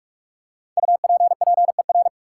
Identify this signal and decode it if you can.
Morse code, heard as a keyed tone, about 46 wpm, 700 Hz. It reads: UPPER
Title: Five Letter Words - 45wpm